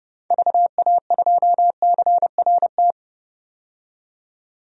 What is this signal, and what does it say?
Morse code, heard as a keyed tone, about 31 wpm, 700 Hz. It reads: VA2CRT
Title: DXCC-100 Call Signs - 6 Characters - 30wpm